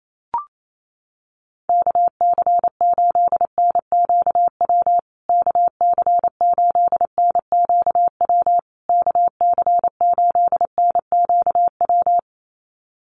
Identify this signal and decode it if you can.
Morse code, heard as a keyed tone, about 28 wpm, 700 Hz. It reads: KC8NQW KC8NQW KC8NQW
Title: USA Call Signs Repeated 3x 28wpm